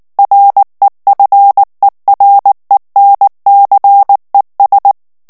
Morse code, heard as a keyed tone, about 19 wpm, 800 Hz. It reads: REFERENCES
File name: ANGLAIS texte clair 3 vitesse 18 mots minute